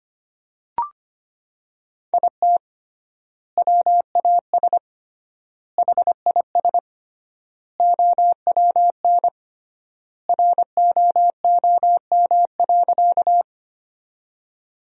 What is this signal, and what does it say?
Morse code, heard as a keyed tone, about 25 wpm, 700 Hz. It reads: IT WAS HIS OWN ROOM.
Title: 3x Word-Spacing - Sentences from Top 300 Words - 25wpm